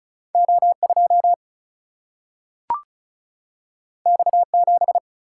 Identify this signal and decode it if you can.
Morse code, heard as a keyed tone, about 36 wpm, 700 Hz. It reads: O 2 X 7
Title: Single-Letter Single-Number 35wpm